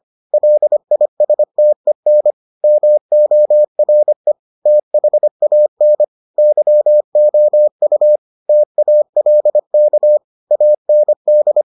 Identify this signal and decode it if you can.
Morse code, heard as a keyed tone, about 25 wpm, 600 Hz. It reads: LISTEN MORE THAN YOU TALK AND